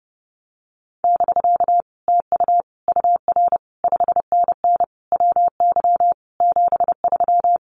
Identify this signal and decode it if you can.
Morse code, heard as a keyed tone, about 30 wpm, 700 Hz. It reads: <BK> TU UR 5NN WY 73
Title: Common QSO Phrases Encore 30wpm